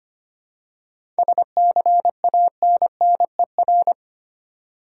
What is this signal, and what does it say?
Morse code, heard as a keyed tone, about 30 wpm, 700 Hz. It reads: SCANNER
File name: practice / 25wpm / Difficult Words - Speed-Racing with Hints - 25wpm